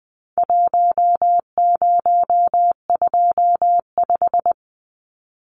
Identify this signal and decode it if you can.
Morse code, heard as a keyed tone, about 20 wpm, 700 Hz. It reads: 1025
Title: Numbers - Contest Style - 4-Digit 20wpm